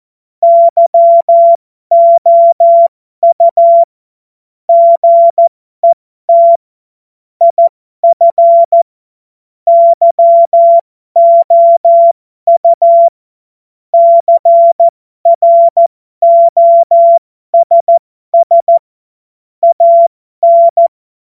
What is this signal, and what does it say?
Morse code, heard as a keyed tone, about 14 wpm, spaced as for 12 wpm, 680 Hz. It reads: YOU GET IF YOU CROSS AN
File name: Morsekurs / Witze